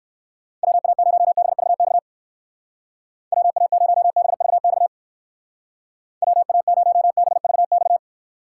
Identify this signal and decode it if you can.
Morse code, heard as a keyed tone, about 55 wpm, 700 Hz. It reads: WA0BVX WA0BVX WA0BVX
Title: Speed Racing - USA Call Signs 45wpm